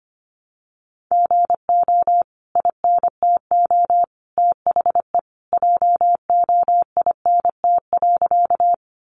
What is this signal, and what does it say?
Morse code, heard as a keyed tone, about 25 wpm, 700 Hz. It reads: GO INTO THE JOINT.
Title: Sentences from Top 1300 Words 25wpm